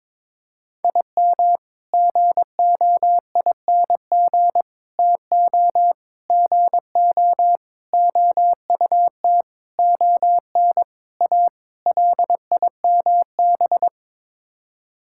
Morse code, heard as a keyed tone, about 22 wpm, 700 Hz. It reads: IM GOING TO GO OUT ON A LIMB
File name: practice / 22wpm / Sets of 8 Words - 22wpm